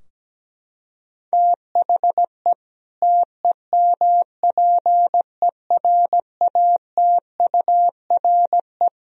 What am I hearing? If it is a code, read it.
Morse code, heard as a keyed tone, about 17 wpm, 700 Hz. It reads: THE TEMPERATURE
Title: Sentences from Top 1000 Words - Review Part 4 - 17wpm